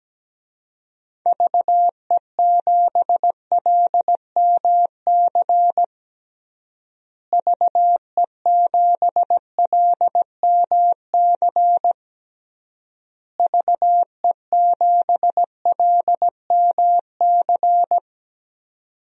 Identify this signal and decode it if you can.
Morse code, heard as a keyed tone, about 17 wpm, 700 Hz. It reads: VE7LMC VE7LMC VE7LMC
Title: DXCC-100 Call Signs - 6 Characters - Repeated 3x - 17wpm